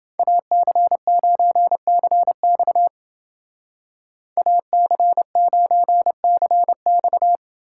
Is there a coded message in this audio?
Yes: Morse code from 0.2 to 7.4 s, heard as a keyed tone, about 28 wpm, 700 Hz. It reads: AC9CX AC9CX